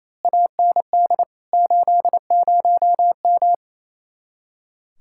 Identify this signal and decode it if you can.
Morse code, heard as a keyed tone, about 31 wpm, 700 Hz. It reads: AND 80M
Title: Speed-Racing Common QSO Phrases - 28wpm